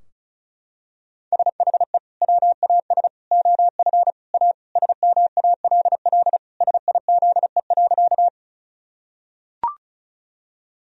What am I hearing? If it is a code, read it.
Morse code, heard as a keyed tone, about 36 wpm, 700 Hz. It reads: SHE WAS OF A SMALL SIZE.
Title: Sentences from Top 1000 Words - Review Part 2 - 35wpm